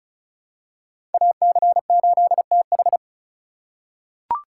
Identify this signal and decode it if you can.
Morse code, heard as a keyed tone, about 36 wpm, 700 Hz. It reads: AC8TH
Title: USA Call Signs - 5 Characters - 2x Character Spacing - 35wpm